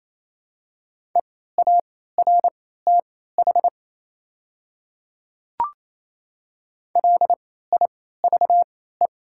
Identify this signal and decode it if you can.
Morse code, heard as a keyed tone, about 28 wpm, spaced as for 16 wpm, 700 Hz. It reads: EARTH LIVE
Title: 3x Character Spacing - Top 200 Words - 28wpm